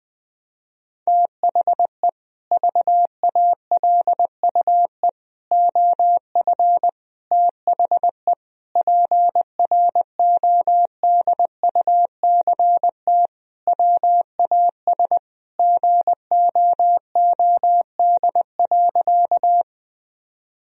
Morse code, heard as a keyed tone, about 20 wpm, 700 Hz. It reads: THE VALUE OF THE PRODUCT WAS GOOD.
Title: Sentences from Top 1000 Words - Review Part 3 - 20wpm